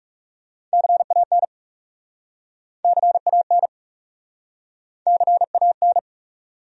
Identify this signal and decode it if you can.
Morse code, heard as a keyed tone, about 40 wpm, 700 Hz. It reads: CAN CAN CAN